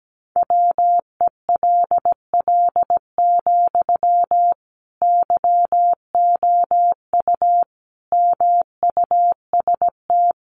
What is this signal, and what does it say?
Morse code, heard as a keyed tone, about 17 wpm, 700 Hz. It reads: WELL, YOU MUST
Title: Sentences from Top 700 Words 17wpm